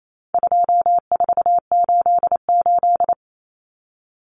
Morse code, heard as a keyed tone, about 28 wpm, 700 Hz. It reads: 2488
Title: Common Cut Numbers - 4-Digit 28wpm